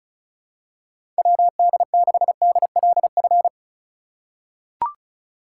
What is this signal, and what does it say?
Morse code, heard as a keyed tone, about 34 wpm, 700 Hz. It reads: WD6DLF
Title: USA Call Signs - 4x Character Spacing - 35wpm